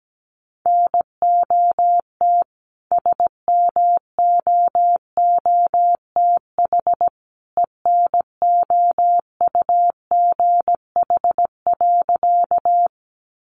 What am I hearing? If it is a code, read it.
Morse code, heard as a keyed tone, about 17 wpm, 700 Hz. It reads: NOT SMOOTH ENOUGH.
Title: Sentences from Top 1400 Words 17wpm